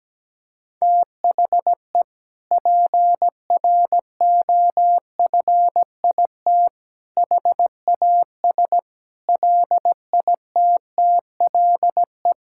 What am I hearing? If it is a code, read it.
Morse code, heard as a keyed tone, about 17 wpm, 700 Hz. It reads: THE PROFIT HAS LITTLE
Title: Sentences from Top 1000 Words - Review Part 5 - 17wpm